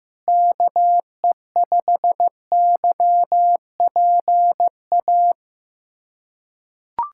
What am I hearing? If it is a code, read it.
Morse code, heard as a keyed tone, about 15 wpm, 700 Hz. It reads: KE5YPA
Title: USA Callsign DE Callsign 15wpm